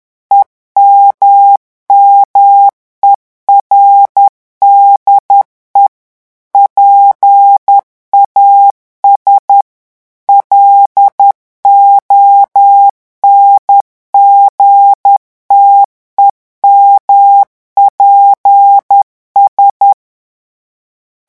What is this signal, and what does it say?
Morse code, heard as a keyed tone, about 10 wpm, 800 Hz. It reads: EMMERDE PAS LONGTEMPS
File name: revision9 citations humoristiques vitesse 10 mots minute